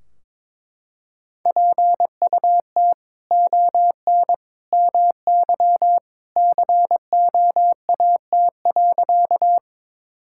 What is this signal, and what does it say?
Morse code, heard as a keyed tone, about 21 wpm, 700 Hz. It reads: PUT ON MY COAT.
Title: Sentences from Top 1600 Words 22wpm